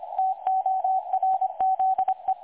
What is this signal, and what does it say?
Morse code, heard as a keyed tone, about 25 wpm, 760 Hz, with radio noise. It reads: TORZE
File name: websdr recording 2018-03-25T20 27 14Z 3565.0kHz